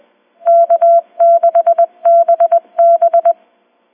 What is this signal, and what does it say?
Morse code, heard as a keyed tone, about 20 wpm, 680 Hz. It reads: K6BB